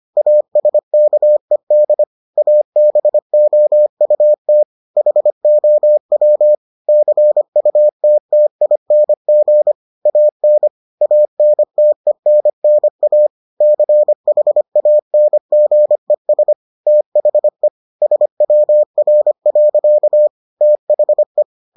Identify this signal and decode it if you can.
Morse code, heard as a keyed tone, about 25 wpm, 600 Hz. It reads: ASKED ABOUT HOW CUTTING AN ANTENNA CHANGES THE SWR. THE